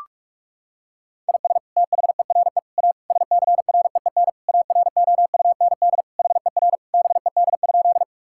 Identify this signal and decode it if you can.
Morse code, heard as a keyed tone, about 42 wpm, 700 Hz. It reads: IS THERE A SCREEN AROUND HER BED?
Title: Sentences from Top 1000 Words - Review Part 6 - 45wpm